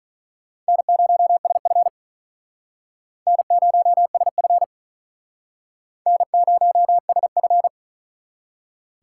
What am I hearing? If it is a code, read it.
Morse code, heard as a keyed tone, about 47 wpm, 700 Hz. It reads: N0SF N0SF N0SF